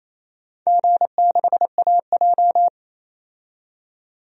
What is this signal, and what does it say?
Morse code, heard as a keyed tone, about 29 wpm, 700 Hz. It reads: G6AJ
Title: DXCC-100 Call Signs - 4 Characters - 28wpm